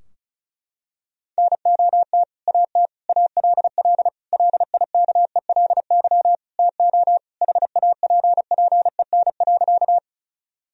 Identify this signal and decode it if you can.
Morse code, heard as a keyed tone, about 35 wpm, 700 Hz. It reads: NOT AT ALL LIKELY TO HAPPEN.
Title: Sentences from Top 1000 Words - Review Part 4 - 35wpm